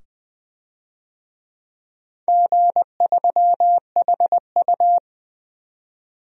Morse code, heard as a keyed tone, about 20 wpm, 700 Hz. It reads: G3HU
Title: DXCC-100 Call Signs - 4 Characters - 20wpm